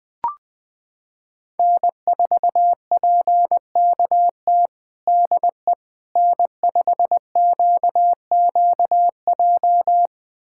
Morse code, heard as a keyed tone, about 20 wpm, 700 Hz. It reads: N4PKT DE N5QQJ
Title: USA Callsign DE Callsign - 5 Characters - 20wpm